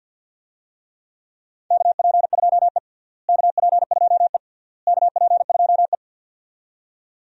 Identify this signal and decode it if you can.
Morse code, heard as a keyed tone, about 49 wpm, 700 Hz. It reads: KP2E KP2E KP2E
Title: DXCC-100 Call Signs - 4 Characters - Repeated 3x - 50wpm